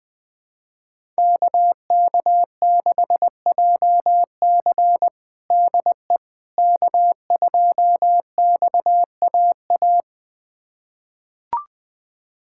Morse code, heard as a keyed tone, about 20 wpm, 700 Hz. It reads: KK6JC DE K2XAA
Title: USA Callsign DE Callsign - 5 Characters - 20wpm